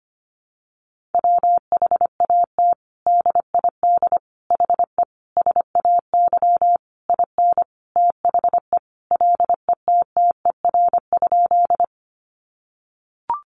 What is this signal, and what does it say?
Morse code, heard as a keyed tone, about 25 wpm, 700 Hz. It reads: WHAT DID HE SAY IN THE LETTER?
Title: Sentences from Top 300 Words 25wpm